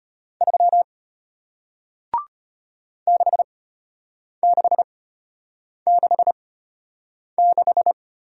Morse code, heard as a keyed tone, about 33 wpm, spaced as for 8 wpm, 690 Hz. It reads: 3 6666